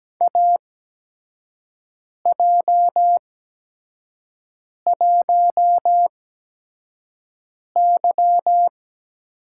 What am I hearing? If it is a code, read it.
Morse code, heard as a keyed tone, about 17 wpm, spaced as for 5 wpm, 700 Hz. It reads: AJ1Y
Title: USA Call Signs - 4 Characters - 8x Character Spacing - 17wpm